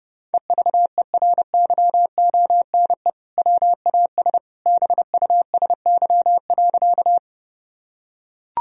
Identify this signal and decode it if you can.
Morse code, heard as a keyed tone, about 30 wpm, 700 Hz. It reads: EVERYONE WAS BUSY.
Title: Sentences from Top 1000 Words - Review Part 3 - 30wpm